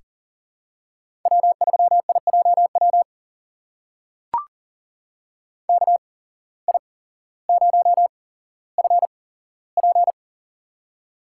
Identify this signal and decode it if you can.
Morse code, heard as a keyed tone, about 40 wpm, spaced as for 17 wpm, 700 Hz. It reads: W3IJW KI0FP
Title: USA Call Signs - 5 Characters - 8x Character Spacing - 40wpm